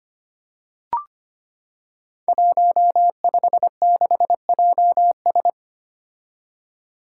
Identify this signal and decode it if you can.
Morse code, heard as a keyed tone, about 25 wpm, 700 Hz. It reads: 156JS
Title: Five Challenging Letter-Number Combinations 25wpm